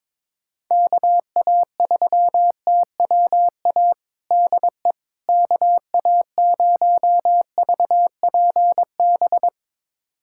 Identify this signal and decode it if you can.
Morse code, heard as a keyed tone, about 22 wpm, 700 Hz. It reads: KA3TWA DE KA0VPB
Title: USA Callsign DE Callsign - 6 Characters - 22wpm